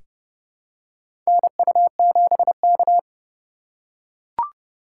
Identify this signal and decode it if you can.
Morse code, heard as a keyed tone, about 29 wpm, 700 Hz. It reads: NU7K
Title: USA Call Signs - 4 Characters - 2x Character Spacing - 30wpm